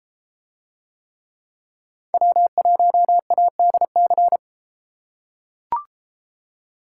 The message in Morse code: W1ADC

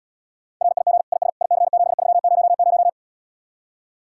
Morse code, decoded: RIG IS IC7300

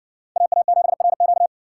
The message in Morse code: AA7RQ

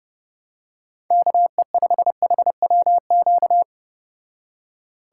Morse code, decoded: KE5HWQ

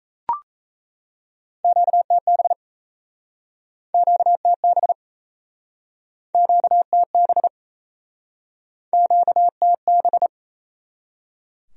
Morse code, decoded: QTB QTB QTB QTB

